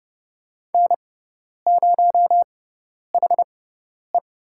Morse code, heard as a keyed tone, about 29 wpm, spaced as for 10 wpm, 700 Hz. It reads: N0HE
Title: USA Call Signs - 4 Characters - 6x Character Spacing - 30wpm